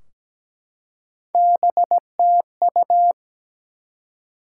Morse code, heard as a keyed tone, about 17 wpm, 700 Hz. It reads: BTU
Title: Top 180 QSO Elements 17wpm